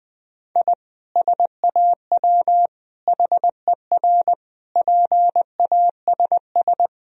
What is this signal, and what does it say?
Morse code, heard as a keyed tone, about 20 wpm, 700 Hz. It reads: I SAW HER PASS
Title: Sentences from Top 1000 Words - Review Part 2 - 20wpm